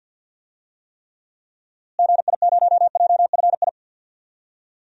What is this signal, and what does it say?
Morse code, heard as a keyed tone, about 53 wpm, 700 Hz. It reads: GI0JFI
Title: DXCC-100 Call Signs - 6 Characters - Repeated 3x - 50wpm